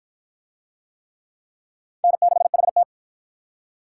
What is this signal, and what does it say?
Morse code, heard as a keyed tone, about 50 wpm, 700 Hz. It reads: N6HT